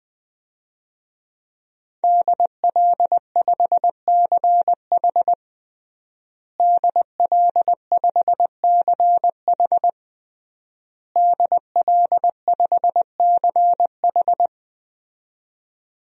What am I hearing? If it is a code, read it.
Morse code, heard as a keyed tone, about 20 wpm, 700 Hz. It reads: DL5CH DL5CH DL5CH
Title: DXCC-100 Call Signs - 5 Characters - Repeated 3x - 20wpm